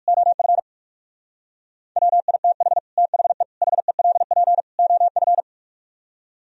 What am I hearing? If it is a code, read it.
Morse code, heard as a keyed tone, about 64 wpm, spaced as for 54 wpm, 700 Hz. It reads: OF WITH THE HELP OF
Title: Speed-Racing - Sets of 4 Words 45wpm